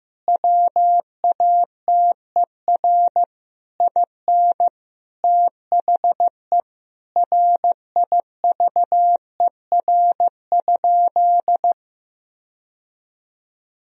Morse code, heard as a keyed tone, about 16 wpm, 700 Hz. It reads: WATER IN THE RIVER?
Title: Sentences from Top 1000 Words - Review Part 4 - 15wpm